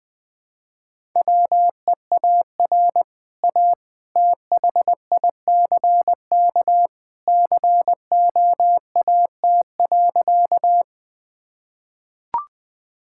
Morse code, heard as a keyed tone, about 19 wpm, 700 Hz. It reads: WEAR A THICK COAT.
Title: Easy Sentences with Hints - 20wpm